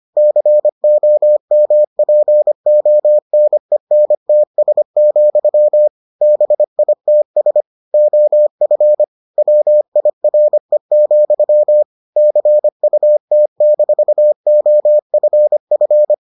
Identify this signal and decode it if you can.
Morse code, heard as a keyed tone, about 25 wpm, 600 Hz. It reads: COMPONENTS, BITS OF WIRE, CUT-OFF